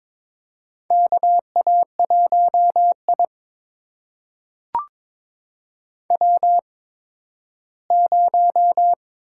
Morse code, heard as a keyed tone, about 22 wpm, spaced as for 13 wpm, 700 Hz. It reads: KA1I W0
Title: USA Call Signs - 8x Character Spacing - 22wpm